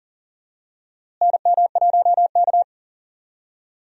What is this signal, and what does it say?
Morse code, heard as a keyed tone, about 40 wpm, 700 Hz. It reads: NM1K
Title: USA Call Signs - 4 Characters - Repeated 3x - 40wpm